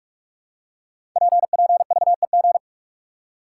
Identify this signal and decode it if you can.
Morse code, heard as a keyed tone, about 44 wpm, 700 Hz. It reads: PPVEG
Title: Code Groups - 5 Letter:Numbers - 45wpm